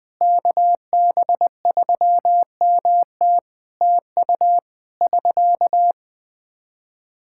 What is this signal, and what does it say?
Morse code, heard as a keyed tone, about 21 wpm, 700 Hz. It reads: KB3MT TU <SK>
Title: Practice QSO - Exchange 4 - Ending - 20wpm